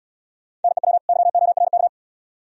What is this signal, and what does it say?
Morse code, heard as a keyed tone, about 78 wpm, 700 Hz. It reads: NEW YORK